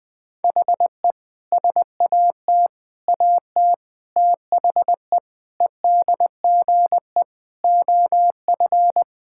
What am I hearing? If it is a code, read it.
Morse code, heard as a keyed tone, about 20 wpm, 700 Hz. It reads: HE SAT AT THE EDGE OF
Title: Sentences with 3-Word Phrases - 20wpm